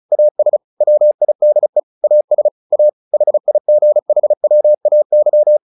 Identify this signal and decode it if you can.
Morse code, heard as a keyed tone, about 34 wpm, 600 Hz. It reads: AS WIDE AS A HIGHWAY
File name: the old man and the sea 2 35wpm